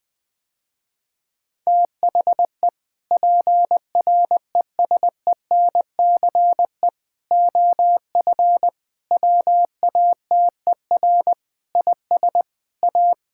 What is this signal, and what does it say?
Morse code, heard as a keyed tone, about 21 wpm, 700 Hz. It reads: THE PRESENCE OF WATER IS A
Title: Sentences with 3-Word Phrases - Spoken Beginning - 20wpm